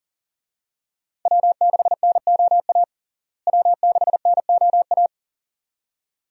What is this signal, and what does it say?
Morse code, heard as a keyed tone, about 40 wpm, 700 Hz. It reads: W6NOA W6NOA
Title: USA Call Signs - 5 Characters - Repeated 2x - 40wpm